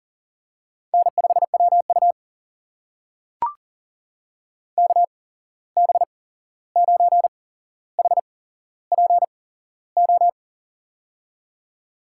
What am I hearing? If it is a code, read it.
Morse code, heard as a keyed tone, about 40 wpm, spaced as for 15 wpm, 700 Hz. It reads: N5WU KB9HPO
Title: USA Call Signs - 8x Character Spacing - 40wpm